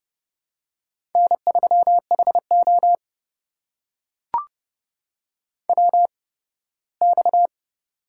Morse code, heard as a keyed tone, about 29 wpm, 700 Hz. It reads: N 3 H O W X
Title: USA Call Signs - 4 Characters - 8x Character Spacing - 30wpm